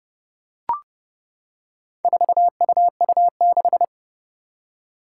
Morse code, heard as a keyed tone, about 30 wpm, 700 Hz. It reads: 4UU6
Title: Four Challenging Letter-Number Combinations 30wpm